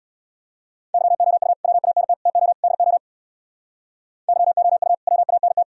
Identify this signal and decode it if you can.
Morse code, heard as a keyed tone, about 60 wpm, 700 Hz. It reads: YOU WANT TO DO YOU WANT
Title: Speed-Racing - Sets of 4 Words 50wpm